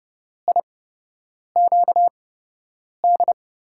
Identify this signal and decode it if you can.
Morse code, heard as a keyed tone, about 30 wpm, spaced as for 8 wpm, 700 Hz. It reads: IQD
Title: USA Call Signs - 6 Characters - 8x Character Spacing - 30wpm